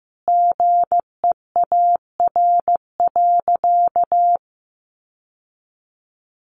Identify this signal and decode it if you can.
Morse code, heard as a keyed tone, about 14 wpm, 700 Hz. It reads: GEAR.
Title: Sentences from Top 1400 Words 15wpm